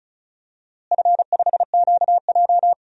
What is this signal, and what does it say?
Morse code, heard as a keyed tone, about 36 wpm, 700 Hz. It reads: F5QJ
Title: DXCC-100 Call Signs - 4 Characters - Repeated 3x - 35wpm